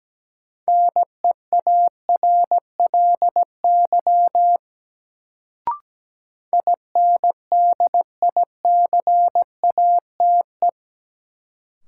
Morse code, heard as a keyed tone, about 17 wpm, 700 Hz. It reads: NEARLY INDICATE
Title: Sentences from Top 1300 Words - New Word Drill - 17wpm